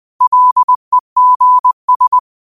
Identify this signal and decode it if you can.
Morse code, heard as a keyed tone, about 20 wpm, 1000 Hz. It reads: LEGS
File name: qotdfiles / 20wpm